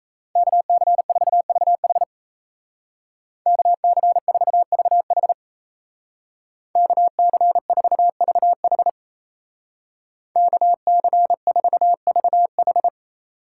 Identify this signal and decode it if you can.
Morse code, heard as a keyed tone, about 35 wpm, 700 Hz. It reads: KC4VH KC4VH KC4VH KC4VH
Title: Speed Racing - USA 2x2 Call Signs 28wpm